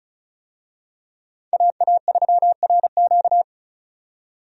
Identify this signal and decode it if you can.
Morse code, heard as a keyed tone, about 34 wpm, 700 Hz. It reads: AA3RQ
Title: USA Call Signs - 5 Characters - 35wpm